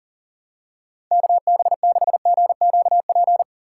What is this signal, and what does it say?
Morse code, heard as a keyed tone, about 37 wpm, 700 Hz. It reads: KB6GQP
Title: USA Call Signs - 6 Characters - 4x Character Spacing - 40wpm